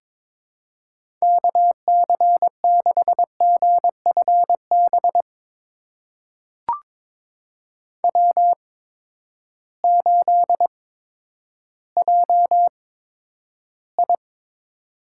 Morse code, heard as a keyed tone, about 22 wpm, spaced as for 11 wpm, 700 Hz. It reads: KC6GFB W8JI
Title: USA Call Signs - 8x Character Spacing - 22wpm